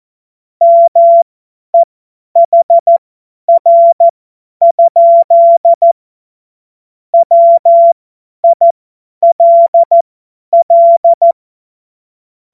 Morse code, heard as a keyed tone, about 14 wpm, spaced as for 10 wpm, 680 Hz. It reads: MEHR? WILL
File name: Morsekurs / Witze / Witz 23. August 2022 50 F70